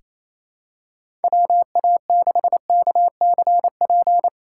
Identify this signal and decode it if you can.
Morse code, heard as a keyed tone, about 28 wpm, 700 Hz. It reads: WA6KCP
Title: USA Call Signs - 6 Characters - 2x Character Spacing - 28wpm